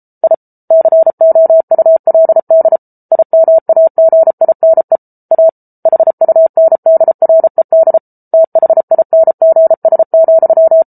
Morse code, heard as a keyed tone, about 34 wpm, 650 Hz. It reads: I COULD IMAGINE A HUNDRED THINGS,